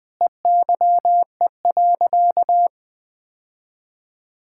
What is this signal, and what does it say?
Morse code, heard as a keyed tone, about 20 wpm, 700 Hz. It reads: EYE.
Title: Sentences with 5-Word Phrases - Spoken Beginning - 20wpm